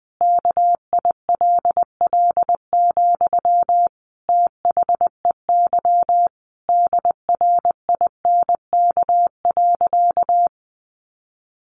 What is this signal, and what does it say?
Morse code, heard as a keyed tone, about 19 wpm, 700 Hz. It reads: KILL, THEY DRINK.
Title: Sentences from Top 1500 Words 20wpm